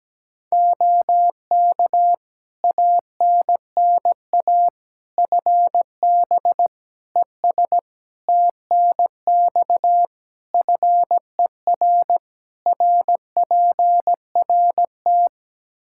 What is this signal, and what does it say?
Morse code, heard as a keyed tone, about 17 wpm, 700 Hz. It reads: OK ANNA FB ES TNX FER RPRT
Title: Practice QSO - Exchange 2 - Info - 17wpm